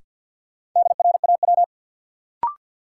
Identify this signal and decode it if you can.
Morse code, heard as a keyed tone, about 53 wpm, 700 Hz. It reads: DRAW
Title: Sentences from Top 300 Words - New Word Drill - 50wpm